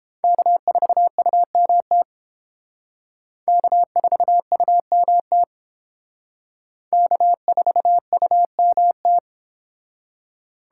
Morse code, heard as a keyed tone, about 30 wpm, 700 Hz. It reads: K4UMT K4UMT K4UMT